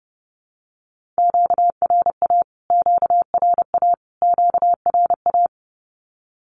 Morse code, heard as a keyed tone, about 30 wpm, 700 Hz. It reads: QRA QRA QRA
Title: Q-Codes Repeated 3x 30wpm